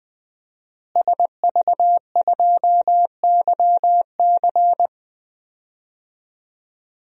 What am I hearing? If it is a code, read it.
Morse code, heard as a keyed tone, about 20 wpm, 700 Hz. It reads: SV2YC
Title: DXCC-100 Call Signs - 5 Characters - Repeated 3x - 20wpm